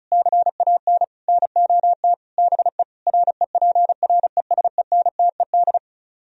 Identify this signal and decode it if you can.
Morse code, heard as a keyed tone, about 40 wpm, 700 Hz. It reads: CAN NOT BE REPRESENTED